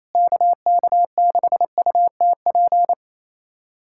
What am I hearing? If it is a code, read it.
Morse code, heard as a keyed tone, about 28 wpm, 700 Hz. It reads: KK6UTP
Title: USA Callsign DE Callsign - 6 Characters - 28wpm